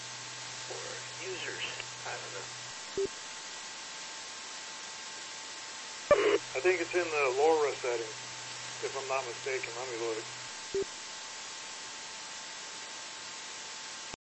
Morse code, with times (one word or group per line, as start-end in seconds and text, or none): none